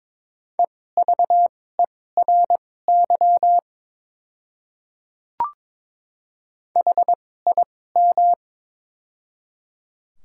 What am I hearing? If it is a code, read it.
Morse code, heard as a keyed tone, about 22 wpm, spaced as for 16 wpm, 700 Hz. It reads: EVERY HIM
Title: 2x Character Spacing - Top 200 Words - 22wpm